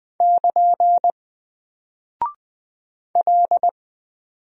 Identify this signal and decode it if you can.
Morse code, heard as a keyed tone, about 20 wpm, 700 Hz. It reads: <KN> L
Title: Single Character (Letter:Number:Punctuation:Prosign) Alt 20wpm